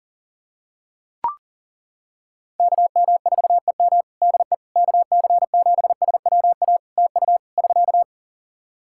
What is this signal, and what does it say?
Morse code, heard as a keyed tone, about 37 wpm, 700 Hz. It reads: KM4EM DE KC7SWA TU <SK>
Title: Practice QSO - Exchange 4 - Ending - 40wpm